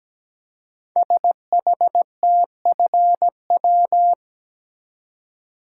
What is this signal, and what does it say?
Morse code, heard as a keyed tone, about 17 wpm, 700 Hz. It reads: SHTFW
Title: Code Groups - 5 Letters - 17wpm